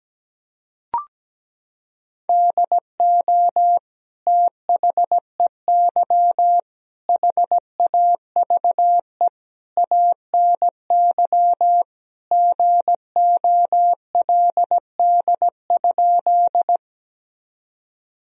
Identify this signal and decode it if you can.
Morse code, heard as a keyed tone, about 17 wpm, 700 Hz. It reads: DO THEY HAVE ANY GOLD?
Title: Sentences from Top 1000 Words - Review Part 3 - 17wpm